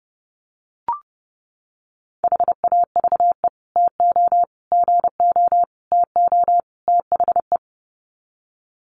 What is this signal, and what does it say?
Morse code, heard as a keyed tone, about 30 wpm, 700 Hz. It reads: HAVE TO GO TO THE
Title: Sets of 5 Words Encore 30wpm